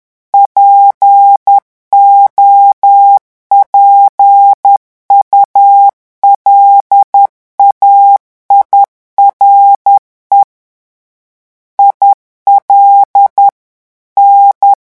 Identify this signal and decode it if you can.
Morse code, heard as a keyed tone, about 10 wpm, 800 Hz. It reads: POPULAIRE IL N
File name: morse mp3 / revision francais 10 mots minute